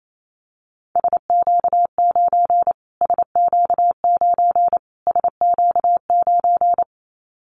Morse code, heard as a keyed tone, about 28 wpm, 700 Hz. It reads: SQ9 SQ9 SQ9
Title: Callsign Prefixes for Top 25 DXCC - Repeated 3x - 28wpm